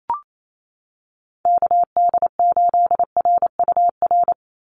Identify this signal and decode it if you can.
Morse code, heard as a keyed tone, about 28 wpm, 700 Hz. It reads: KD8RUR
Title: USA Call Signs 28wpm